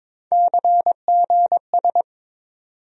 Morse code, heard as a keyed tone, about 22 wpm, 700 Hz. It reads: CGS
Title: USA Call Sign Postfixes - 3 Letters 22wpm